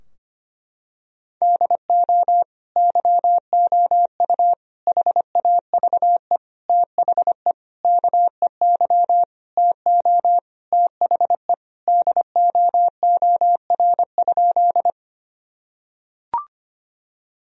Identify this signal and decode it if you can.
Morse code, heard as a keyed tone, about 25 wpm, 700 Hz. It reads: DO YOU HAVE THE KEY TO THE DOOR?
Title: Sentences from Top 1000 Words - Review Part 4 - 25wpm